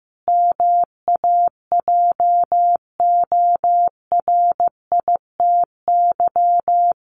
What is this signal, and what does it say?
Morse code, heard as a keyed tone, about 15 wpm, 700 Hz. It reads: MAJORITY
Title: Sets of 5 Words Encore 15wpm